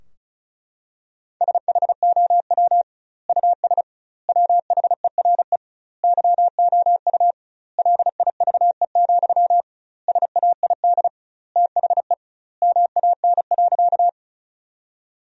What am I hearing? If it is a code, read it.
Morse code, heard as a keyed tone, about 34 wpm, 700 Hz. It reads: SHOW US WHERE YOU LIVE, SAID THE MAN.
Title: 2x Word-Spacing - Sentences from Top 200 Words - 35wpm